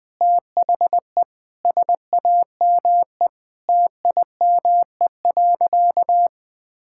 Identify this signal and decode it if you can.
Morse code, heard as a keyed tone, about 20 wpm, 700 Hz. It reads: THE SAME TIME.
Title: Sentences with 4-Word Phrases - 20wpm